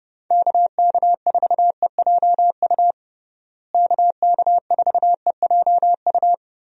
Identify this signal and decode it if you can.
Morse code, heard as a keyed tone, about 32 wpm, 700 Hz. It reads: KK4EJU KK4EJU
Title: USA Call Signs - 6 Characters - Repeated 3x - 30wpm